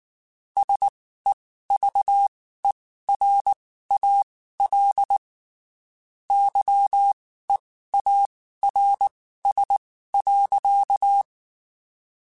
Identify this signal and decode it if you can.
Morse code, heard as a keyed tone, about 19 wpm, spaced as for 14 wpm, 800 Hz. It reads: SEVERAL YEARS.
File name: (spaced) / QSOs Spaced-mp3s / kc7haa @18 sp